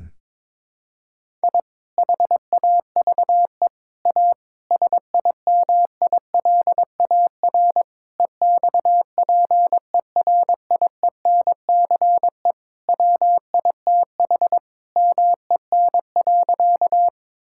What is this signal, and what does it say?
Morse code, heard as a keyed tone, about 22 wpm, 700 Hz. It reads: I HAVE A SIMILAR EXPERIENCE WITH MEN.
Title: Sentences from Top 1000 Words - Review Part 5 - 22wpm